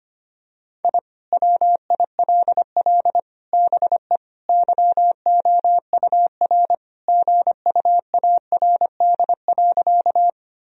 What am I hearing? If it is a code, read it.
Morse code, heard as a keyed tone, about 25 wpm, 700 Hz. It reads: I WILL BE YOUR GUARD.
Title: Sentences from Top 1000 Words - Review Part 4 - 25wpm